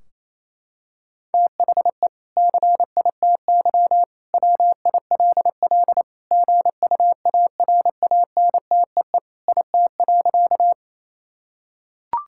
Morse code, heard as a keyed tone, about 28 wpm, 700 Hz. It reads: THE CITY WILL GUARANTEE IT.
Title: Sentences from Top 1600 Words 28wpm